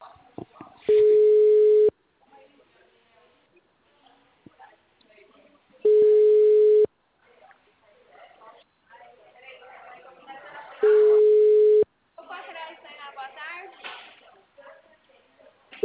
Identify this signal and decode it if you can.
Ringback tone (Europe, 425 Hz), with speech.
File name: out- -2011- - - .2179